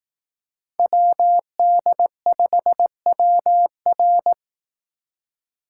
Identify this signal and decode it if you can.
Morse code, heard as a keyed tone, about 18 wpm, 700 Hz. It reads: WD5WR
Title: Speed Racing - USA 2x2 Call Signs 15wpm